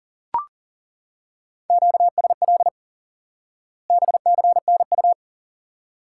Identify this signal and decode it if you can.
Morse code, heard as a keyed tone, about 40 wpm, 700 Hz. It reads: QSL BCNU
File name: IQE Course 3 - Lesson 89 - YAGI - Two Elements - 40wpm